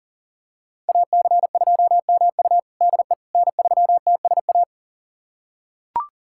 Morse code, heard as a keyed tone, about 37 wpm, 700 Hz. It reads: AC2MU DE N3TSA
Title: USA Callsign DE Callsign - 5 Characters - 40wpm